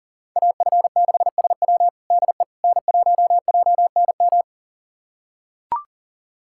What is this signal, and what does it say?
Morse code, heard as a keyed tone, about 40 wpm, 700 Hz. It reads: AF6SW DE N1JNM
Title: USA Callsign DE Callsign - 5 Characters - 40wpm